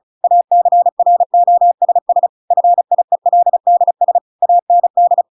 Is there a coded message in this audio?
Yes: Morse code from 0.2 to 5.2 s, heard as a keyed tone, about 36 wpm, 700 Hz. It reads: ACROSS FIELDS AND